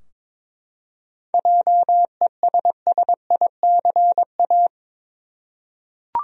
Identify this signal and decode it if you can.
Morse code, heard as a keyed tone, about 22 wpm, 700 Hz. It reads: JESSICA
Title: USA Common Names - 22wpm